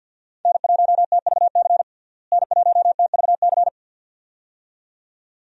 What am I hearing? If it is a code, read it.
Morse code, heard as a keyed tone, about 49 wpm, 700 Hz. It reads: N1TVC N1TVC
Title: USA Call Signs - 5 Characters - Repeated 2x - 50wpm